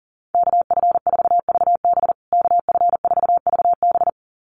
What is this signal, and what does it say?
Morse code, heard as a keyed tone, about 40 wpm, 700 Hz. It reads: KF4VB KF4VB
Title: USA Call Signs Repeated 2x 40wpm